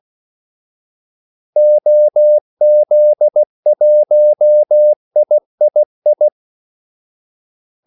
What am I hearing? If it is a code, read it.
Morse code, heard as a keyed tone, about 16 wpm, 600 Hz. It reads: OZ1III
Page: Kaldesignaler 16 wpm